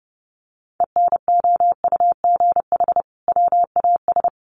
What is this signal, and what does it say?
Morse code, heard as a keyed tone, about 30 wpm, 700 Hz. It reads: ENOUGH WAS
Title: Sentences from Top 1400 Words 30wpm